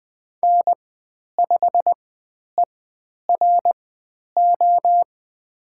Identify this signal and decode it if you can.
Morse code, heard as a keyed tone, about 20 wpm, spaced as for 10 wpm, 700 Hz. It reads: N5ERO
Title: 102 Lesson 17g - Letter u - Call Signs 20wpm